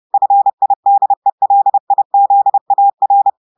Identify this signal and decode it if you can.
Morse code, heard as a keyed tone, about 30 wpm, 800 Hz. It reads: FIDELIZAR
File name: portugais texte en clair 2 vitesse 30 mots minutes